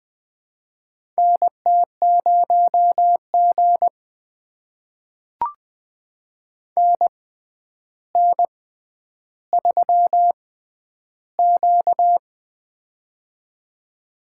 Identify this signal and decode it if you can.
Morse code, heard as a keyed tone, about 20 wpm, spaced as for 10 wpm, 700 Hz. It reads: NT0G NN3Q
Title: USA Call Signs - 4 Characters - 6x Character Spacing - 20wpm